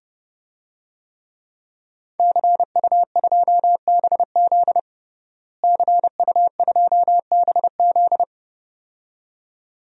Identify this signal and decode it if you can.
Morse code, heard as a keyed tone, about 32 wpm, 700 Hz. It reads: CU2BZ CU2BZ
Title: DXCC-100 Call Signs - 5 Characters - Repeated 2x - 30wpm